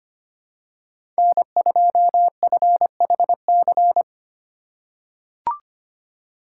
Morse code, heard as a keyed tone, about 25 wpm, 700 Hz. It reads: N2FHC
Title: USA Call Signs - 5 Characters - 2x Character Spacing - 25wpm